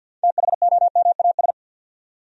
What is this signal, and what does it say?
Morse code, heard as a keyed tone, about 50 wpm, 700 Hz. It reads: THOMAS
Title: USA Common Names - 50wpm